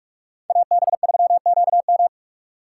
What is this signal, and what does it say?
Morse code, heard as a keyed tone, about 46 wpm, 700 Hz. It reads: AB3QM
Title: USA Call Signs - 5 Characters - 6x Character Spacing - 45wpm